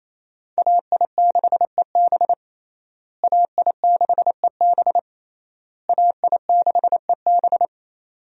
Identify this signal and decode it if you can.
Morse code, heard as a keyed tone, about 28 wpm, 700 Hz. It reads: AI6EB AI6EB AI6EB
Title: USA Call Signs - 5 Characters - Repeated 3x - 28wpm